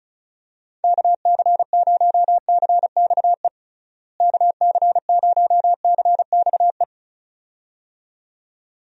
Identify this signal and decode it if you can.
Morse code, heard as a keyed tone, about 36 wpm, 700 Hz. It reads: KC0CXE KC0CXE
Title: USA Call Signs - 6 Characters - Repeated 2x - 35wpm